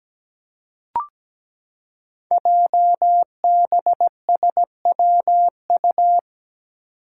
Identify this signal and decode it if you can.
Morse code, heard as a keyed tone, about 17 wpm, 690 Hz. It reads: JBSWU
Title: Code Groups - 5 Letter:Numbers - 17wpm